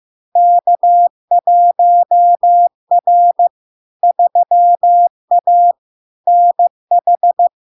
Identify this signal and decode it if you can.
Morse code, heard as a keyed tone, about 15 wpm, 700 Hz. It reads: K1R 3A NH
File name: LARK-FD-2022-Log-in-CW-15wpm-700-Hz